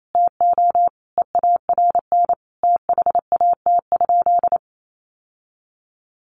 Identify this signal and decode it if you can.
Morse code, heard as a keyed tone, about 28 wpm, 700 Hz. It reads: TO EARN THAT?
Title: Sentences from Top 1500 Words 28wpm